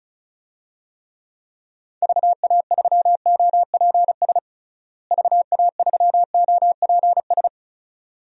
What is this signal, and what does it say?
Morse code, heard as a keyed tone, about 36 wpm, 700 Hz. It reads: VA3OPS VA3OPS
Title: DXCC-100 Call Signs - 6 Characters - Repeated 2x - 35wpm